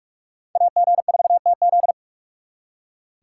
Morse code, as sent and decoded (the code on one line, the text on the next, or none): .- --. ....- - --..
AG4TZ